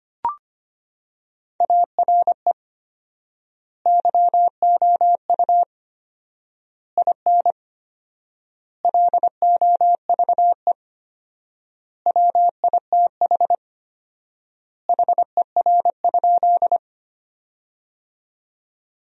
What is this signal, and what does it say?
Morse code, heard as a keyed tone, about 25 wpm, 700 Hz. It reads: ARE YOU IN LOVE WITH HER?
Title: 4x Word-Spacing - Sentences from Top 400 Words - 25wpm